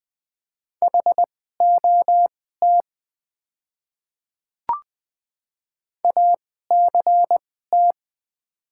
Morse code, heard as a keyed tone, about 20 wpm, spaced as for 15 wpm, 690 Hz. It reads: HOT ACT
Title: 2x Character Spacing - Top 200 Words - 20wpm